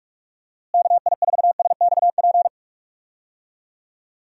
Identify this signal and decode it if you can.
Morse code, heard as a keyed tone, about 46 wpm, 700 Hz. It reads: KI4SXP
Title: USA Call Signs - 6 Characters - Repeated 3x - 45wpm